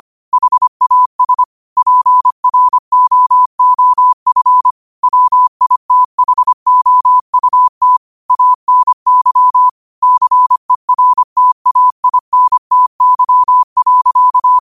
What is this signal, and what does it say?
Morse code, heard as a keyed tone, about 25 wpm, 1000 Hz. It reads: HAS PROOF WITHOUT ANY CERTAINTY.
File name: qotdfiles / 25wpm